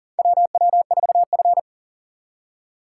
Morse code, heard as a keyed tone, about 40 wpm, 700 Hz. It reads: WW4F
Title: USA Call Signs - 4 Characters - 2x Character Spacing - 40wpm